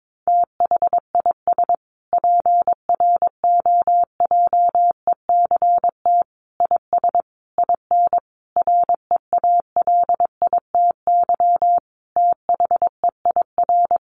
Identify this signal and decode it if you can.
Morse code, heard as a keyed tone, about 22 wpm, 700 Hz. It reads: THIS PROJECT IS IN REALITY THEIR
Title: Sentences from Top 1400 Words 22wpm